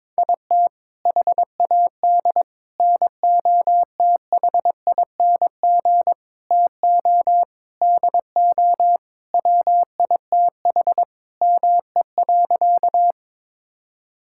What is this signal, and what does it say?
Morse code, heard as a keyed tone, about 22 wpm, 700 Hz. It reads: IT HAD NOTHING TO DO WITH ME.
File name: practice / 22wpm / Sentences with 4-Word Phrases - 22wpm